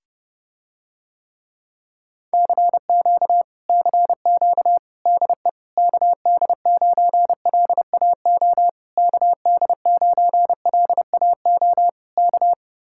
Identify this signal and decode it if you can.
Morse code, heard as a keyed tone, about 30 wpm, 700 Hz. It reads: CQ CQ DE KD9LAO KD9LAO K